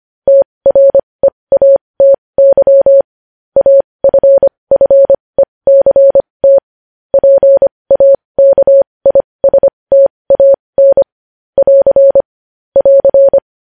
Morse code, heard as a keyed tone, about 25 wpm, spaced as for 20 wpm, 550 Hz. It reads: TREATY AFFECT PAKISTAN <AR> <AR>
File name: podcast / News headlines-1x-25x20